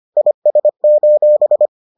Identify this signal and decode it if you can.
Morse code, heard as a keyed tone, about 25 wpm, 600 Hz. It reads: IS: